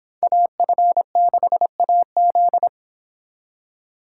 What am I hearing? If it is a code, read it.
Morse code, heard as a keyed tone, about 25 wpm, 700 Hz. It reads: AF6AZ